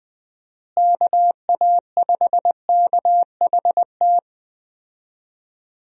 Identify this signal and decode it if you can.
Morse code, heard as a keyed tone, about 20 wpm, 700 Hz. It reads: KA5KHT
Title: USA Call Signs - 6 Characters - Repeated 3x - 20wpm